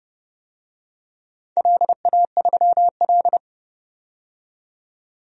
Morse code, heard as a keyed tone, about 30 wpm, 700 Hz. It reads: LA3L
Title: DXCC-100 Call Signs - 4 Characters - 30wpm